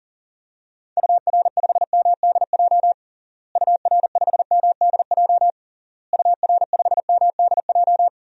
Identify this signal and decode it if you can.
Morse code, heard as a keyed tone, about 37 wpm, 700 Hz. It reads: UR5MDJ UR5MDJ UR5MDJ
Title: DXCC-100 Call Signs - 6 Characters - Repeated 3x - 40wpm